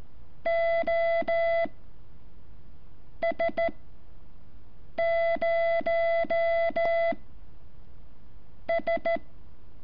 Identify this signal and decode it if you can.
Morse code, heard as a keyed tone, about 14 wpm, 670 Hz. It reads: O S 0 S
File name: curso / ejercicio 11